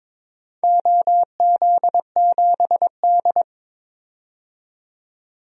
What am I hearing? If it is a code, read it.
Morse code, heard as a keyed tone, about 22 wpm, 700 Hz. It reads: OZ7D
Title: DXCC-100 Call Signs - 4 Characters - 22wpm